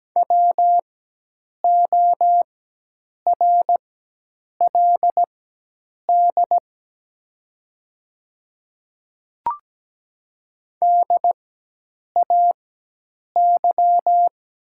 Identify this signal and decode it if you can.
Morse code, heard as a keyed tone, about 17 wpm, spaced as for 8 wpm, 700 Hz. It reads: WORLD DAY
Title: 4x Character Spacing - Top 200 Words - 17wpm